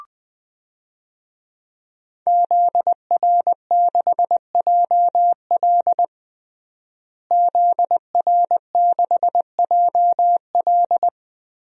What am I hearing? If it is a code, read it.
Morse code, heard as a keyed tone, about 20 wpm, 700 Hz. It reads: ZR6JL ZR6JL
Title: DXCC-100 Call Signs - 5 Characters - Repeated 2x - 20wpm